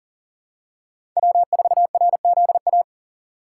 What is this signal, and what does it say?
Morse code, heard as a keyed tone, about 37 wpm, 700 Hz. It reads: W4RZA
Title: USA Call Signs - 3x Character Spacing - 40wpm